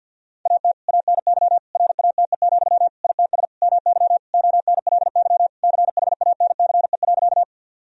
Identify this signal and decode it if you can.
Morse code, heard as a keyed tone, about 52 wpm, 700 Hz. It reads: AT ANY RATE, ITS MY ONLY CHANCE.
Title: Sentences from Top 1000 Words - Review Part 4 - 50wpm